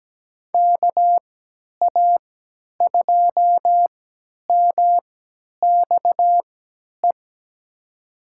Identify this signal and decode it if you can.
Morse code, heard as a keyed tone, about 17 wpm, spaced as for 10 wpm, 700 Hz. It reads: KA2MXE
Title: USA Call Signs - 6 Characters - 3x Character Spacing - 17wpm